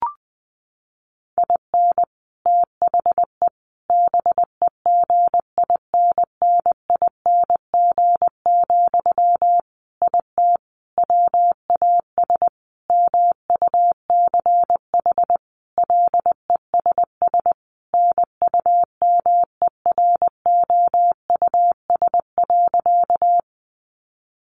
Morse code, heard as a keyed tone, about 20 wpm, 700 Hz. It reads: IN THE BEGINNING, IT WAS MUCH LESS NUMEROUS.
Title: Sentences from Top 1400 Words 20wpm